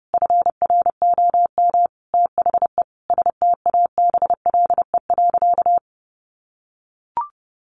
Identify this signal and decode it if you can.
Morse code, heard as a keyed tone, about 30 wpm, 700 Hz. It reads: FROM THE STABLE.
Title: Sentences from Top 1500 Words 30wpm